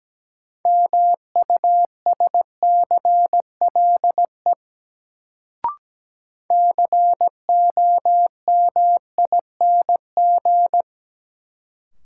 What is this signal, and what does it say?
Morse code, heard as a keyed tone, about 17 wpm, 700 Hz. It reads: MUSCLE COMING
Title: Sentences from Top 1300 Words - New Word Drill - 17wpm